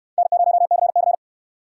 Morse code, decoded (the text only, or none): N0CC